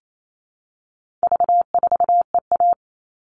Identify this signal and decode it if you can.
Morse code, heard as a keyed tone, about 28 wpm, 700 Hz. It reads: V4EA
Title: All Cut Numbers - 4-Digit 28wpm